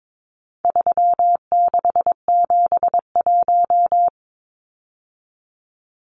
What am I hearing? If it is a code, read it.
Morse code, heard as a keyed tone, about 22 wpm, 700 Hz. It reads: 3671
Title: Numbers - 4-Digit 22wpm